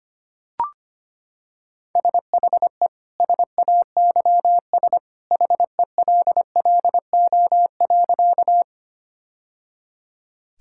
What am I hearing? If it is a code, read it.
Morse code, heard as a keyed tone, about 25 wpm, 700 Hz. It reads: SHE SAYS HELLO.
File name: practice / 25wpm / Sentences with 2-Word Phrases - 25wpm